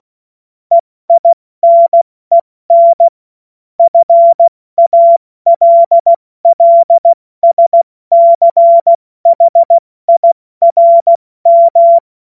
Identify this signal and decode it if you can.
Morse code, heard as a keyed tone, about 16 wpm, 680 Hz. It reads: EINEN FALLSCHIRM
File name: Morsekurs / Witze / Witz 16. Februar 2021, 70 F80